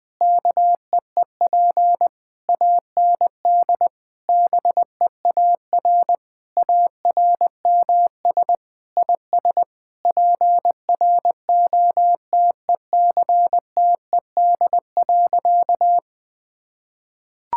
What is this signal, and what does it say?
Morse code, heard as a keyed tone, about 21 wpm, 700 Hz. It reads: KEEP AND BEAR ARMS IS PROTECTED.
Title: Sentences with 9-Word Phrases - 20wpm